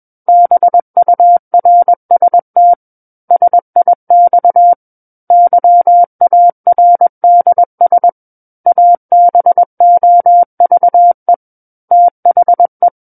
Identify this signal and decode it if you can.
Morse code, heard as a keyed tone, about 21 wpm, 700 Hz. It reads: BURST SIX YARDS ABOVE THE
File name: Book One / 12-Chapter 12 - 21 WPM